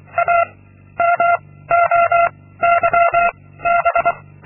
Morse code, heard as a keyed tone, about 24 wpm, 690 Hz. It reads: A M O Y B
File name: Linksmakalnis-2